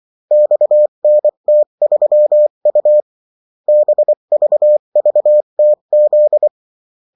Morse code, heard as a keyed tone, about 24 wpm, 600 Hz. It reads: XNT3U BVVTZ
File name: morse mp3 / revision9-vitesse 22 mots